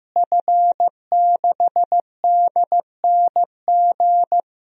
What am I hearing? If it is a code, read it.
Morse code, heard as a keyed tone, about 15 wpm, 700 Hz. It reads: F6DNG
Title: DXCC-100 Call Signs - 5 Characters - Repeated 3x - 15wpm